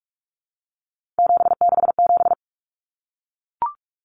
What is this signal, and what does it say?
Morse code, heard as a keyed tone, about 44 wpm, 700 Hz. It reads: 767
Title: Common Cut Numbers - 3-Digit 45wpm